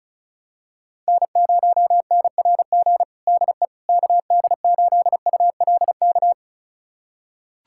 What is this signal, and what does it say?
Morse code, heard as a keyed tone, about 36 wpm, 700 Hz. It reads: N0NRG DE KD8ULK
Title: USA Callsign DE Callsign 35wpm